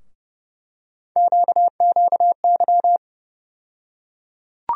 Morse code, heard as a keyed tone, about 31 wpm, 700 Hz. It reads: QQY
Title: USA Call Sign Postfixes - 3 Letters 30wpm